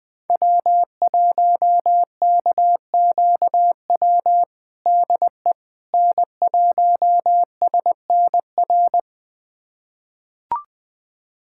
Morse code, heard as a keyed tone, about 20 wpm, 700 Hz. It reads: W1KQW DE N1SNR
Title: USA Callsign DE Callsign - 5 Characters - 20wpm